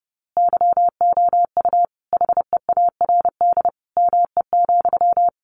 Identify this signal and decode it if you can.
Morse code, heard as a keyed tone, about 30 wpm, 700 Hz. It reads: YOU HEARD ME,
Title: A Sound Of Thunder 30wpm